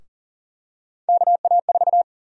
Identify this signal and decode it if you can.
Morse code, heard as a keyed tone, about 40 wpm, 700 Hz. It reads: KA4
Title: USA Call Sign Prefixes - Repeated 2x - 40wpm